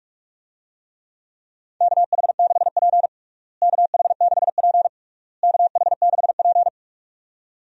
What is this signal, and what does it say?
Morse code, heard as a keyed tone, about 46 wpm, 700 Hz. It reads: KH6P KH6P KH6P
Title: DXCC-100 Call Signs - 4 Characters - Repeated 3x - 45wpm